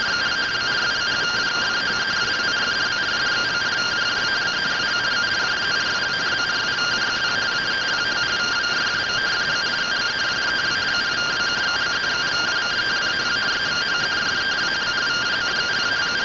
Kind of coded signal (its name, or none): two-tone teleprinter signal